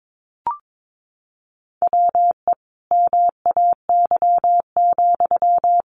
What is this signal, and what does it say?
Morse code, heard as a keyed tone, about 22 wpm, 700 Hz. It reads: WE MAY,
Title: Sentences from Top 1500 Words 22wpm